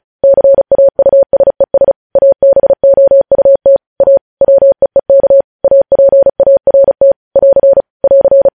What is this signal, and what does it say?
Morse code, heard as a keyed tone, about 36 wpm, 550 Hz. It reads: CAUSES ABOUT A WEEK APART <AR> <AR>
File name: podcast / News headlines-1x-35x35